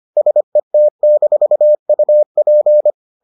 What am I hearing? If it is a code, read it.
Morse code, heard as a keyed tone, about 25 wpm, 600 Hz. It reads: SET-UP